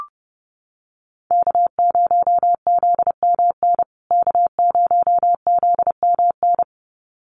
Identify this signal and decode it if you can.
Morse code, heard as a keyed tone, about 30 wpm, 700 Hz. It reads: K0ZMN K0ZMN
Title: USA Call Signs Repeated 2x 30wpm